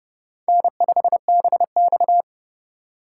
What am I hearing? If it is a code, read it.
Morse code, heard as a keyed tone, about 31 wpm, 690 Hz. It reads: N5BX
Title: USA Call Signs - 4x Character Spacing - 30wpm